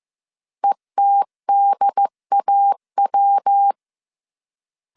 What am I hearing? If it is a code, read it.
Morse code, heard as a keyed tone, about 15 wpm, 800 Hz. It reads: ETDAW
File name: CW / DET / 1 3 1500C S A 14wpm 2447